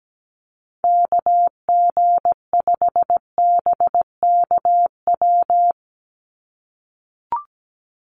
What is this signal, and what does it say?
Morse code, heard as a keyed tone, about 17 wpm, 700 Hz. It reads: KG5BKW
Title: USA Call Signs 17wpm